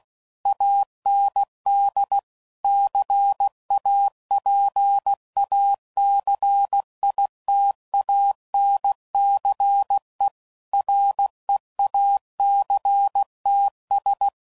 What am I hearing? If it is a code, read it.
Morse code, heard as a keyed tone, about 16 wpm, 800 Hz. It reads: AND CAPACITANCE REACTS
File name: CW Code Groups @15WPM / Technical Subjects / Reactance